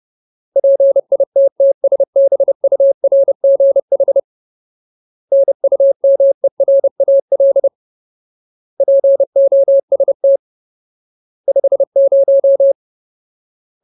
Morse code, heard as a keyed tone, about 30 wpm, 550 Hz. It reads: PITTSBURGH NUMERAL POST 50
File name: Ultimate Word List / 30WPM Ultimate